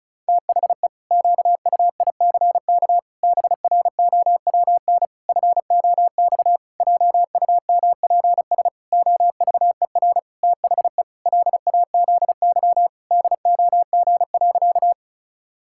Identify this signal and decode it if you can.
Morse code, heard as a keyed tone, about 34 wpm, 700 Hz. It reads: THE QUICK BROWN FOX JUMPS OVER THE LAZY DOG.
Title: Pangram Challenge v2 - 35wpm